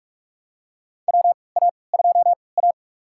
Morse code, heard as a keyed tone, about 46 wpm, spaced as for 25 wpm, 700 Hz. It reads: WA2A
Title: USA Call Signs - 3x Character Spacing - 45wpm